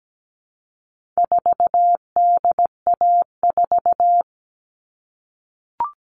Morse code, heard as a keyed tone, about 17 wpm, 700 Hz. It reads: 4DA4
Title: All Cut Numbers - 4-Digit 17wpm